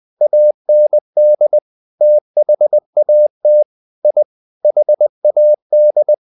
Morse code, heard as a keyed tone, about 20 wpm, 600 Hz. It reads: AND THAT I HAD
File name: the old man and the sea part 2 20wpm